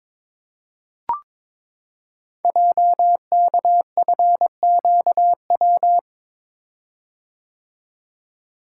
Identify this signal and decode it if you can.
Morse code, heard as a keyed tone, about 22 wpm, 700 Hz. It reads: JKFQW
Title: Code Groups - 5 Letters - 22wpm